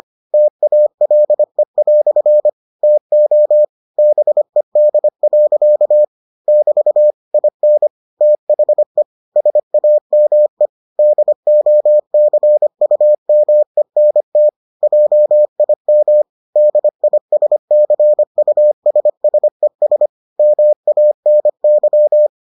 Morse code, heard as a keyed tone, about 25 wpm, 600 Hz. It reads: TALE" TO BED. <BT> IN THE SAME DOCUMENT JIM DISCUSSES MANY